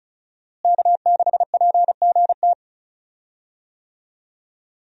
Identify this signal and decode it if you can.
Morse code, heard as a keyed tone, about 36 wpm, 700 Hz. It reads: K6PGT
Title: USA Call Signs - 5 Characters - Repeated 3x - 35wpm